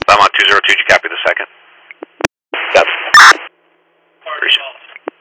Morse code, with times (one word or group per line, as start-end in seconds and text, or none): none